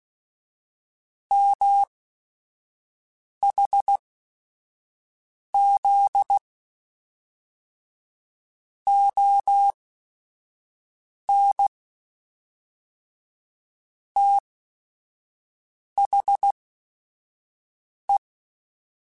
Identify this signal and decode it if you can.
Morse code, heard as a keyed tone, about 16 wpm, spaced as for 5 wpm, 800 Hz. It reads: MHZ ON THE